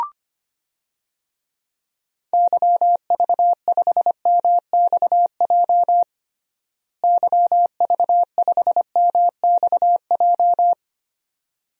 Morse code, heard as a keyed tone, about 25 wpm, 700 Hz. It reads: YV5MXJ YV5MXJ
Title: DXCC-100 Call Signs - 6 Characters - Repeated 2x - 25wpm